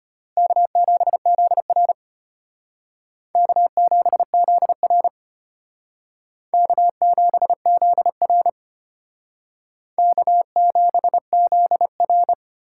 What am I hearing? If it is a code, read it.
Morse code, heard as a keyed tone, about 32 wpm, 700 Hz. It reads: K7ZR K7ZR K7ZR K7ZR